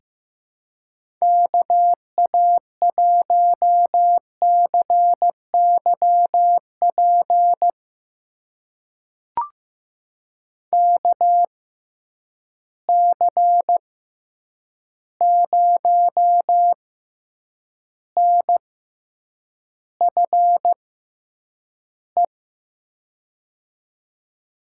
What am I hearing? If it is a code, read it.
Morse code, heard as a keyed tone, about 15 wpm, spaced as for 8 wpm, 700 Hz. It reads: KA1CYP KC0NFE
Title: USA Call Signs - 6 Characters - 6x Character Spacing - 15wpm